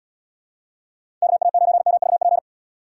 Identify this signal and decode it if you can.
Morse code, heard as a keyed tone, about 65 wpm, 700 Hz. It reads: KA9MVP